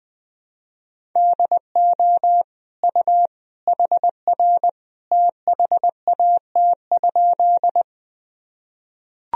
Morse code, heard as a keyed tone, about 21 wpm, 700 Hz. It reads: DO U HR THAT?
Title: Sentences from Top 300 Words and Abbreviations 20wpm